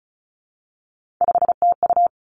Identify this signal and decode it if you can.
Morse code, heard as a keyed tone, about 36 wpm, 700 Hz. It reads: 5TU
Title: Three Challenging Letter-Number Combinations 35wpm